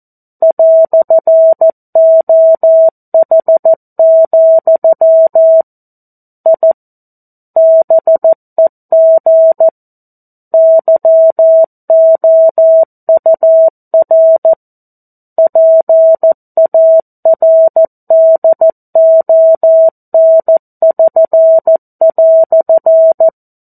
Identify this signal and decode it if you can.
Morse code, heard as a keyed tone, about 14 wpm, 650 Hz. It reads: "OH, I BEG YOUR PARDON<SN>"
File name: 02-Chapter-02 14 WPM